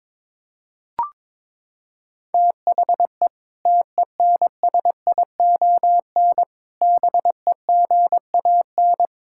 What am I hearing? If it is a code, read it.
Morse code, heard as a keyed tone, about 22 wpm, 700 Hz. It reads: THE TENSION BEGAN
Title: Sentences from Top 1600 Words 22wpm